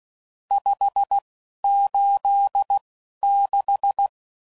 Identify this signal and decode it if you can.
Morse code, heard as a keyed tone, about 16 wpm, spaced as for 11 wpm, 800 Hz. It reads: 586
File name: CW Code Groups @10WPM / RSTs / Various RST No's